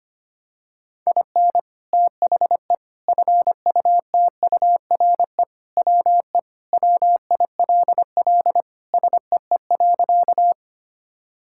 Morse code, heard as a keyed tone, about 25 wpm, 700 Hz. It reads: IN THE FUTURE WE WILL SEE.
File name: practice / 25wpm / Sentences with 3-Word Phrases - 25wpm